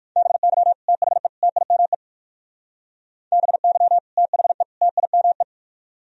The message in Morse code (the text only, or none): BY THE TIME BY THE TIME